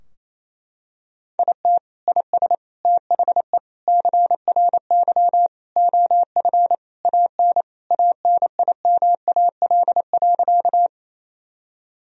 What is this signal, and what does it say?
Morse code, heard as a keyed tone, about 28 wpm, 700 Hz. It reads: IT IS THE CRY OF AN ANIMAL.
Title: Sentences from Top 1000 Words - Review Part 2 - 28wpm